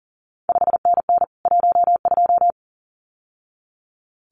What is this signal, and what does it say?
Morse code, heard as a keyed tone, about 41 wpm, 700 Hz. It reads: 5NN 12
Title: CQ WW DX Contest Exchange Encore - 40wpm